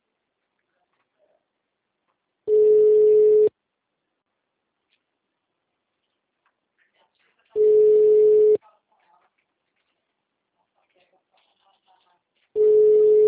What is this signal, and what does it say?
Ringback tone (Europe, 425 Hz), with speech.
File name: out- -2008- - - .39792